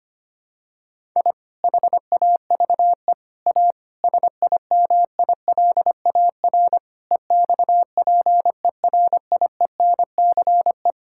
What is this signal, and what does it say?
Morse code, heard as a keyed tone, about 25 wpm, 700 Hz. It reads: I HAVE A SIMILAR EXPERIENCE
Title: Sentences from Top 1000 Words - Review Part 5 - 25wpm